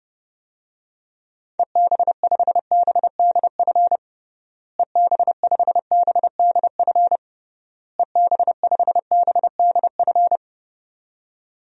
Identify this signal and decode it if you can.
Morse code, heard as a keyed tone, about 32 wpm, 700 Hz. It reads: EB5BDF EB5BDF EB5BDF
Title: DXCC-100 Call Signs - 6 Characters - Repeated 3x - 30wpm